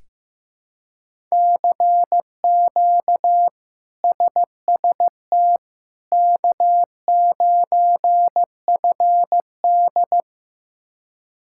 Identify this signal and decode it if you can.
Morse code, heard as a keyed tone, about 15 wpm, 700 Hz. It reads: CQ SST K9FD
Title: K1USN SST QSO Practice - 15wpm